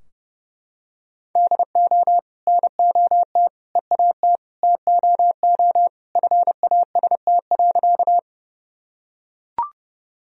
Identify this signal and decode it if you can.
Morse code, heard as a keyed tone, about 32 wpm, 700 Hz. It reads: DO NOT EAT TOO FAST.
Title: Sentences from Top 1000 Words - Review Part 2 - 30wpm